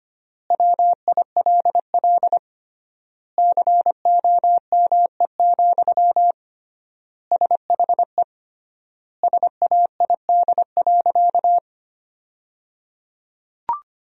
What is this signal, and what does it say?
Morse code, heard as a keyed tone, about 25 wpm, 700 Hz. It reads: WILL COME, SHE SAID.
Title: 3x Word-Spacing - Sentences from Top 100 Words - 25wpm